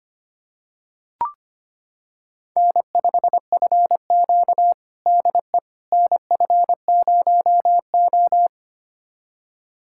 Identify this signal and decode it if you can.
Morse code, heard as a keyed tone, about 25 wpm, 700 Hz. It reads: N5FQ DE NF0O
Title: USA Callsign DE Callsign - 4 Characters - 25wpm